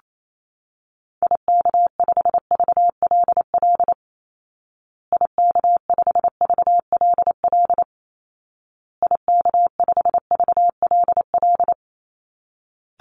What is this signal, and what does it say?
Morse code, heard as a keyed tone, about 28 wpm, 700 Hz. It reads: IK5VLL IK5VLL IK5VLL
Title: Call Signs Repeated 2x - 28wpm